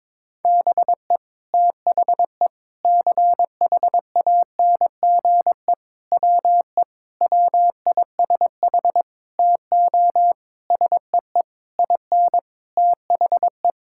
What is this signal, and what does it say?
Morse code, heard as a keyed tone, about 22 wpm, 700 Hz. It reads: BE THE CHANGE WE WISH TO SEE IN THE
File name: Sets of 9 Words - 22wpm